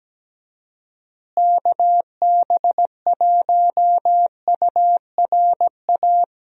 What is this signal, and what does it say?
Morse code, heard as a keyed tone, about 17 wpm, 700 Hz. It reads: KB1URA
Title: USA Call Signs - 6 Characters - Repeated 3x - 17wpm